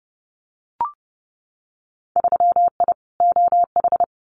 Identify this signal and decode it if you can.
Morse code, heard as a keyed tone, about 30 wpm, 700 Hz. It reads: 3I OH
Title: Winter Field Day Exchanges Encore 30wpm